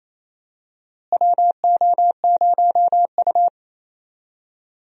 Morse code, heard as a keyed tone, about 28 wpm, 700 Hz. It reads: WO0U
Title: USA Call Signs - 4 Characters - Repeated 3x - 28wpm